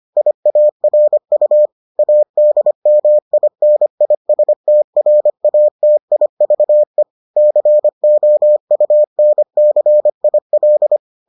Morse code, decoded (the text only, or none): IARU ADMINISTRATIVE COUNCIL